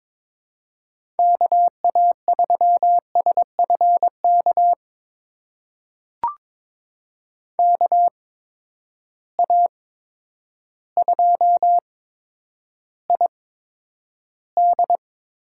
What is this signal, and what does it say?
Morse code, heard as a keyed tone, about 22 wpm, spaced as for 10 wpm, 700 Hz. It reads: KA3SFK KA2ID
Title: USA Call Signs - 6 Characters - 8x Character Spacing - 22wpm